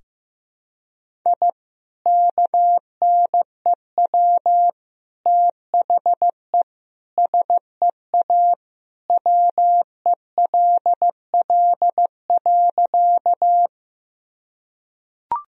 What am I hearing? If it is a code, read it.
Morse code, heard as a keyed tone, about 15 wpm, 700 Hz. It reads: I KNEW THE SEA WELL.
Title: Sentences from Top 1000 Words - Review Part 1 - 15wpm